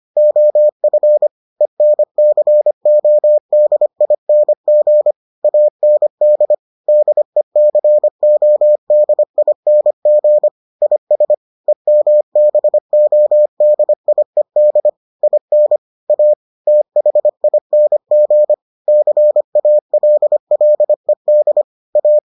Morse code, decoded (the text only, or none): OF ENCODING AND DECODING IS EMBODIED IN A THING CALLED A